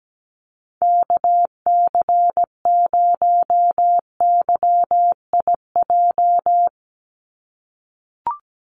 Morse code, heard as a keyed tone, about 17 wpm, 700 Hz. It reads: KC0YIJ
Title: USA Call Signs 17wpm